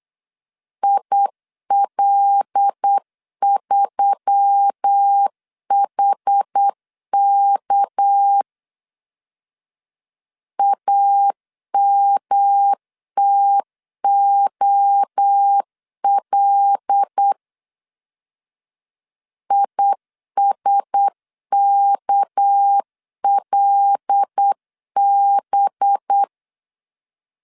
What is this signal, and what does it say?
Morse code, heard as a keyed tone, about 8 wpm, 800 Hz. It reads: IL3HK AMTOL ISKLB
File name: LEZIONI / 6 PKX0 / 6 PK 8wpm 3003